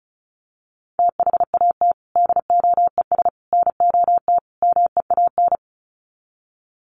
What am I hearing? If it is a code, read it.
Morse code, heard as a keyed tone, about 34 wpm, 700 Hz. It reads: THAT DOES NOT MEAN
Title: Sets of 4 Words 35wpm